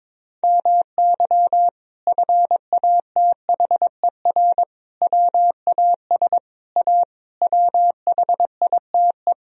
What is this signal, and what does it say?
Morse code, heard as a keyed tone, about 22 wpm, 700 Hz. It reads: MY FATHER WAS A WHITE
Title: Sentences from Top 1000 Words - Review Part 1 - 22wpm